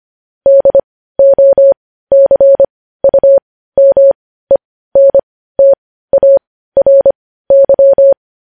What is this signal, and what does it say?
Morse code, heard as a keyed tone, about 25 wpm, spaced as for 15 wpm, 550 Hz. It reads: DOCUMENTARY
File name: podcast / News headlines-1x-25x15